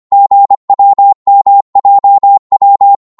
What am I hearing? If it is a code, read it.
Morse code, heard as a keyed tone, about 25 wpm, 800 Hz. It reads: GWMJW
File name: Morse / ltr-25wpm